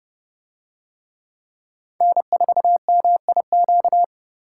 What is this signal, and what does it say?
Morse code, heard as a keyed tone, about 29 wpm, 700 Hz. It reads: N4MIQ